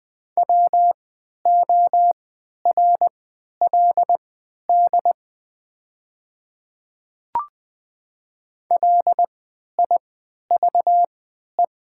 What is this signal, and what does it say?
Morse code, heard as a keyed tone, about 20 wpm, spaced as for 11 wpm, 700 Hz. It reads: WORLD LIVE
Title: 3x Character Spacing - Top 500 Words - 20wpm